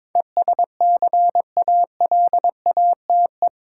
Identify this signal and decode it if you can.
Morse code, heard as a keyed tone, about 22 wpm, 700 Hz. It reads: ESCALATE
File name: practice / 22wpm / Eight Letter Words - 22wpm